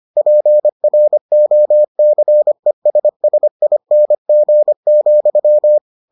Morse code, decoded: PROCESSING,

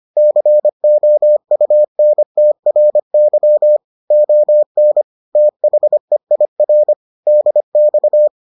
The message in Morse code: COUNTRY ON THEIR DX

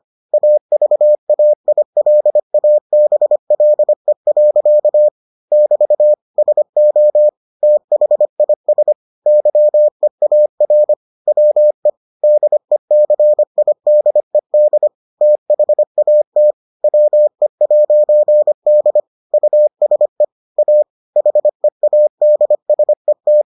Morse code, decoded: AVAILABLE. <BT> SO THIS YEAR WE DECIDED THAT WE'D USE A HEADSET

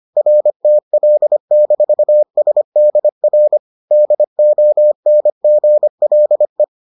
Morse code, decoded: RTL-SDR DONGLE

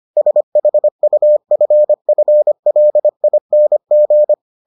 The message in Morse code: SHUFFLING

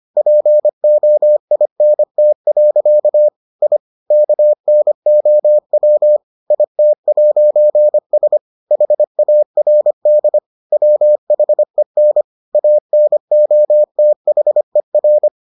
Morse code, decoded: POINT. I KNOW IT'S HARD WHEN ANOTHER

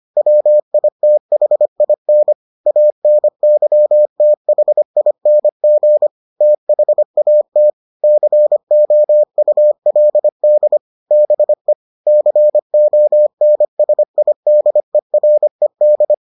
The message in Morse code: WITHIN ANYTHING THAT COULD BE CONSIDERED

